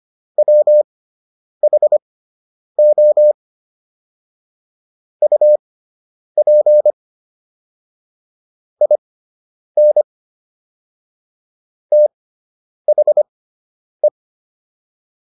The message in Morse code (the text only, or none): WHO UP IN THE